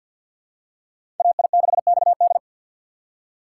Morse code, heard as a keyed tone, about 51 wpm, 700 Hz. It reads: AI6XD
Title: USA Call Signs - 5 Characters - 2x Character Spacing - 50wpm